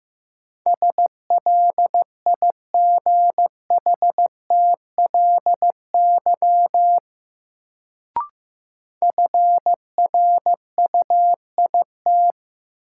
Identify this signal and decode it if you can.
Morse code, heard as a keyed tone, about 15 wpm, 700 Hz. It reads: SLIGHTLY FRUIT
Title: Sentences from Top 1300 Words - New Word Drill - 15wpm